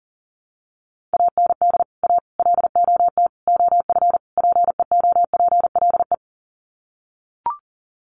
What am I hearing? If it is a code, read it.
Morse code, heard as a keyed tone, about 40 wpm, 700 Hz. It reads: AND A LOT OF PEOPLE
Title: Sets of 5 Words 40wpm